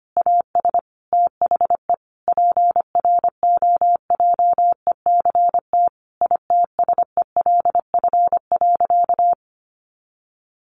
Morse code, heard as a keyed tone, about 25 wpm, 700 Hz. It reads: AS THE PROJECT ITSELF.
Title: Sentences from Top 1400 Words 25wpm